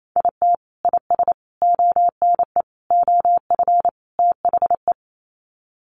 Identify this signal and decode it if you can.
Morse code, heard as a keyed tone, about 28 wpm, 700 Hz. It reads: IT IS ONE OF THE
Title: Sets of 5 Words 28wpm